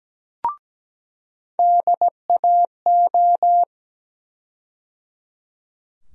Morse code, heard as a keyed tone, about 17 wpm, 700 Hz. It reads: DAO
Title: USA Call Sign Postfixes - 3 Letters 17wpm